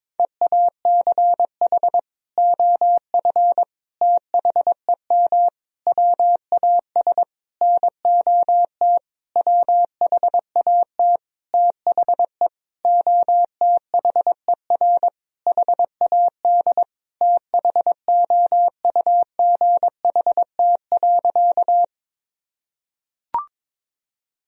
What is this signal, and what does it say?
Morse code, heard as a keyed tone, about 22 wpm, 700 Hz. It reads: EACH OF THEM WAS NOT WHAT THE OTHER HAD THOUGHT.
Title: Sentences from Top 1000 Words - Review Part 1 - 22wpm